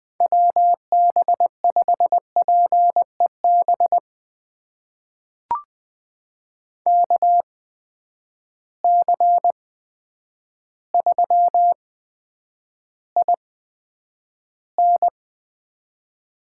Morse code, heard as a keyed tone, about 20 wpm, spaced as for 9 wpm, 700 Hz. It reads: WB5PEB KC3IN
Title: USA Call Signs - 6 Characters - 8x Character Spacing - 20wpm